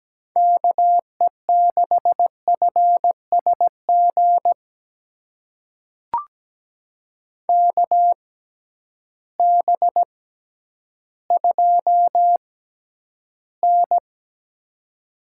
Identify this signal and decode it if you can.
Morse code, heard as a keyed tone, about 17 wpm, spaced as for 10 wpm, 700 Hz. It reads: KE6FSG KB2N
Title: USA Call Signs - 6 Characters - 6x Character Spacing - 17wpm